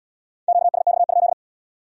Morse code, heard as a keyed tone, about 70 wpm, 700 Hz. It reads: QSX?